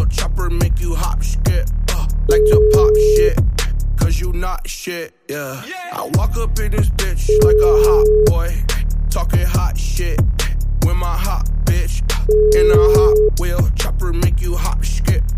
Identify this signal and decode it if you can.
Ringback tone (Europe, 425 Hz).